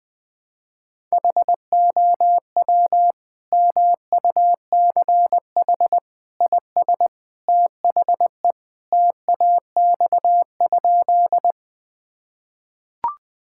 Morse code, heard as a keyed tone, about 19 wpm, 700 Hz. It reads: HOW MUCH IS THE TAX?
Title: Sentences from Top 1000 Words - Review Part 5 - 20wpm